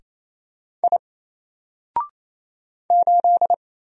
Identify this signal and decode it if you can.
Morse code, heard as a keyed tone, about 27 wpm, 700 Hz. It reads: I 8
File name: Single-Letter-Number-28wpm